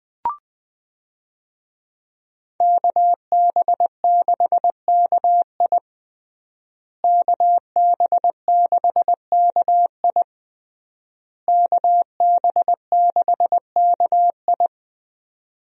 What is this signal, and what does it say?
Morse code, heard as a keyed tone, about 21 wpm, 700 Hz. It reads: KB6KI KB6KI KB6KI
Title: USA Call Signs - 5 Characters - Repeated 3x - 20wpm